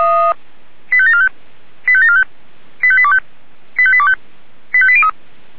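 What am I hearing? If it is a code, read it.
Five-tone selective call (ZVEI), heard as quick runs of five tones, tones of about 70 ms. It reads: 85745 85745 85825 85825 85803
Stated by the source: ZVEI five-tone call received on SDR.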